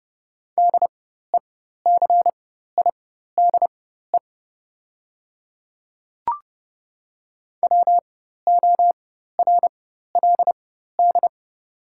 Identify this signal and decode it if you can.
Morse code, heard as a keyed tone, about 30 wpm, spaced as for 14 wpm, 700 Hz. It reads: DECIDE WORLD
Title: 4x Character Spacing - Top 500 Words - 30wpm